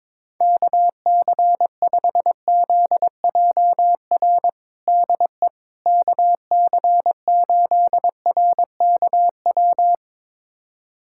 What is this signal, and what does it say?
Morse code, heard as a keyed tone, about 21 wpm, 700 Hz. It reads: KC5ZJR DE KC8RKW
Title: USA Callsign DE Callsign - 6 Characters - 22wpm